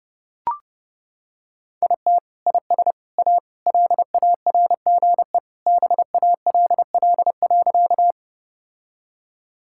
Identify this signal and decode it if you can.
Morse code, heard as a keyed tone, about 30 wpm, 700 Hz. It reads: IT IS A LARGE BALL.
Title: Sentences from Top 1000 Words - Review Part 6 - 30wpm